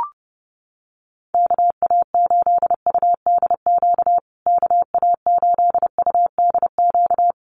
Morse code, heard as a keyed tone, about 28 wpm, 700 Hz. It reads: KA8UDQ KA8UDQ
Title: USA Call Signs Repeated 3x 30wpm